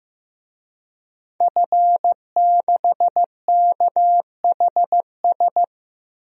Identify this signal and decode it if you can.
Morse code, heard as a keyed tone, about 16 wpm, 700 Hz. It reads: F6KHS
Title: DXCC-100 Call Signs - 5 Characters - Repeated 2x - 15wpm